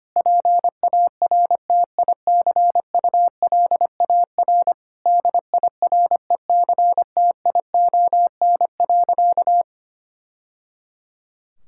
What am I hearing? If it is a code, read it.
Morse code, heard as a keyed tone, about 25 wpm, 700 Hz. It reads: PARTICULAR DIRECTION.
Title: Sentences from Top 1000 Words - Review Part 5 - 25wpm